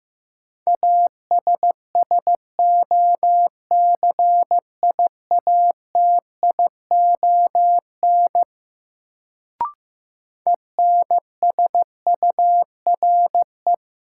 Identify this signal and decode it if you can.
Morse code, heard as a keyed tone, about 15 wpm, 700 Hz. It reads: ASSOCIATION ENSURE
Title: Sentences from Top 1100 Words - New Word Drill - 15wpm